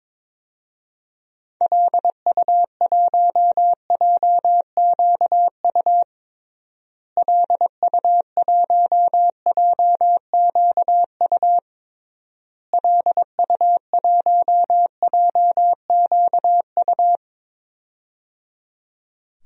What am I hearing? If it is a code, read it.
Morse code, heard as a keyed tone, about 22 wpm, 700 Hz. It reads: LU1JQU LU1JQU LU1JQU
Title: DXCC-100 Call Signs - 6 Characters - Repeated 3x - 22wpm